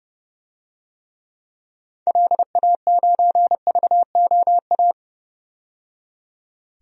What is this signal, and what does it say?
Morse code, heard as a keyed tone, about 30 wpm, 700 Hz. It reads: LA9VOA
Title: DXCC-100 Call Signs - 6 Characters - 30wpm